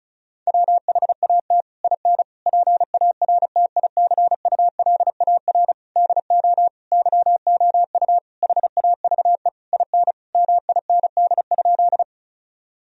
Morse code, heard as a keyed tone, about 34 wpm, 700 Hz. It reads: WHAT IN PARTICULAR DO YOU HAVE IN MIND?
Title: Sentences from Top 1000 Words - Review Part 4 - 35wpm